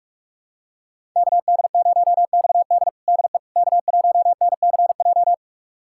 Morse code, heard as a keyed tone, about 46 wpm, 700 Hz. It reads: KD0XD DE K1NCJ
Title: USA Callsign DE Callsign - 5 Characters - 45wpm